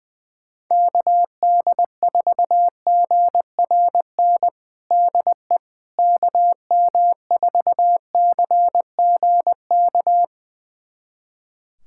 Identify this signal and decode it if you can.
Morse code, heard as a keyed tone, about 21 wpm, 700 Hz. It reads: KD4GRN DE KM4CGK
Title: USA Callsign DE Callsign - 6 Characters - 20wpm